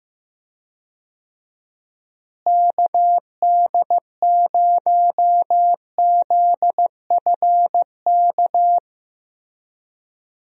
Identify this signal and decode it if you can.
Morse code, heard as a keyed tone, about 15 wpm, 700 Hz. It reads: KD0ZFK
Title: USA Call Signs - 6 Characters - Repeated 3x - 15wpm